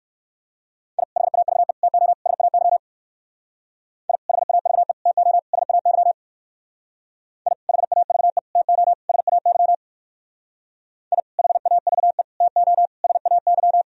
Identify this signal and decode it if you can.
Morse code, heard as a keyed tone, about 58 wpm, 700 Hz. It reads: I HAVE TO SAY I HAVE TO SAY I HAVE TO SAY I HAVE TO SAY